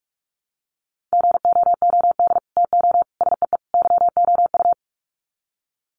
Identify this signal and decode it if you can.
Morse code, heard as a keyed tone, about 44 wpm, 700 Hz. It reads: GOOD TO SEE YOU
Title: Sets of 4 Words 45wpm